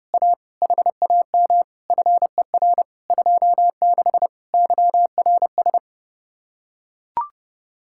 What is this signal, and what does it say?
Morse code, heard as a keyed tone, about 30 wpm, 700 Hz. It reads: A HAM FER 26 YRS